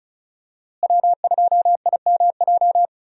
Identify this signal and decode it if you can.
Morse code, heard as a keyed tone, about 36 wpm, 700 Hz. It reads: W2IMJ
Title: USA Call Signs - 5 Characters - Repeated 3x - 35wpm